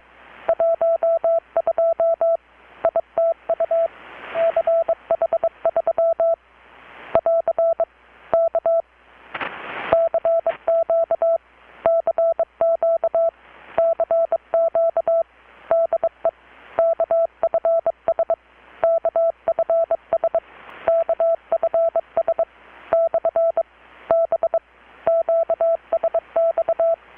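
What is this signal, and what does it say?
Morse code, heard as a keyed tone, about 22 wpm, 660 Hz. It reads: 12 ITU CH3 <AR> K CQ CQ CQ DE KFS KFS KFS / B QSX
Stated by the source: Morse from KPH's Night of Nights XX maritime radiotelegraphy event, received in Chiapas, Mexico about 3,800 km away on a Kenwood R-600 with a grounded 10 m long-wire antenna.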